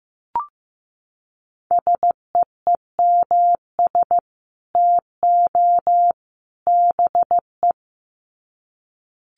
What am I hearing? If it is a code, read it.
Morse code, heard as a keyed tone, about 15 wpm, 700 Hz. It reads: SEEMS TO BE
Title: Sets of 3 Words 15wpm